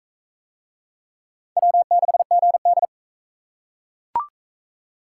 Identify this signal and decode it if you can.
Morse code, heard as a keyed tone, about 42 wpm, 700 Hz. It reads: W6GD